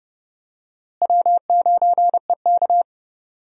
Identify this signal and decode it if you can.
Morse code, heard as a keyed tone, about 30 wpm, 700 Hz. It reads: W9EK
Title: USA Call Signs - 4 Characters - Repeated 2x - 30wpm